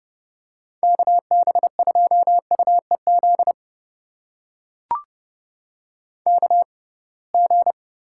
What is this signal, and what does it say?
Morse code, heard as a keyed tone, about 31 wpm, spaced as for 23 wpm, 700 Hz. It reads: KB2UEZ KG
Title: USA Call Signs - 6 Characters - 6x Character Spacing - 30wpm